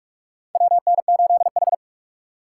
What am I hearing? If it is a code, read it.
Morse code, heard as a keyed tone, about 45 wpm, 700 Hz. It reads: WN8H
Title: USA Call Signs - 4 Characters - 4x Character Spacing - 45wpm